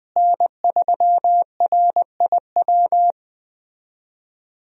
Morse code, heard as a keyed tone, about 20 wpm, 700 Hz. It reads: N3RIW
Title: USA Call Signs - 5 Characters - Repeated 3x - 20wpm